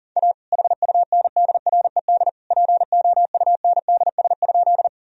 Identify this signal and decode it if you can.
Morse code, heard as a keyed tone, about 37 wpm, 700 Hz. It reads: A HUNDRED POUNDS?
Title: Sentences from Top 1000 Words - Review Part 2 - 40wpm